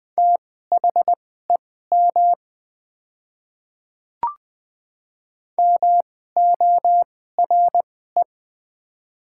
Morse code, heard as a keyed tone, about 21 wpm, spaced as for 15 wpm, 690 Hz. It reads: THEM MORE
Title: 2x Character Spacing - Top 100 Words - 20wpm